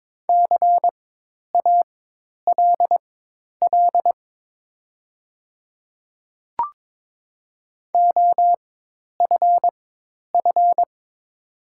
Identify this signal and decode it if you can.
Morse code, heard as a keyed tone, about 22 wpm, spaced as for 10 wpm, 700 Hz. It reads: CALL OFF
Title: 4x Character Spacing - Top 200 Words - 22wpm